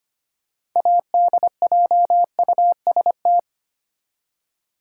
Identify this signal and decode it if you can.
Morse code, heard as a keyed tone, about 25 wpm, 700 Hz. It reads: ADJUST
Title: Six Letter Words - 25wpm